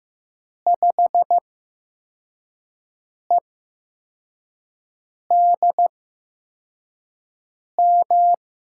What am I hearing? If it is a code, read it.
Morse code, heard as a keyed tone, about 15 wpm, spaced as for 4 wpm, 700 Hz. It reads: 5EDM
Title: USA Call Signs - 6 Characters - 8x Character Spacing - 15wpm